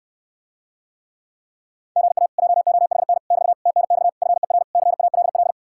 Morse code, heard as a keyed tone, about 69 wpm, 700 Hz. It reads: GA JOHN <BT> TNX FER CALL